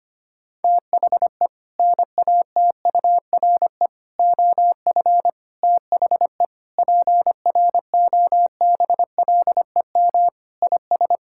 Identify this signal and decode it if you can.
Morse code, heard as a keyed tone, about 25 wpm, 700 Hz. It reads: THE NATURE OF THE PROBLEM IS
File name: practice / 25wpm / Sentences with 4-Word Phrases - 25wpm